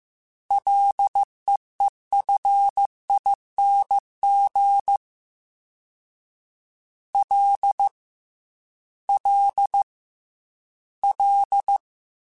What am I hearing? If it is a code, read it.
Morse code, heard as a keyed tone, about 15 wpm, 800 Hz. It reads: LEEFING L L L
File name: F and L / F and L mp3 / F and L Prefixes 3x2 Twice @13